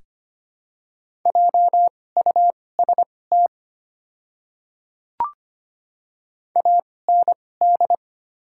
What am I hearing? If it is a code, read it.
Morse code, heard as a keyed tone, about 25 wpm, spaced as for 18 wpm, 700 Hz. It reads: JUST AND
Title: 2x Character Spacing - Top 200 Words - 25wpm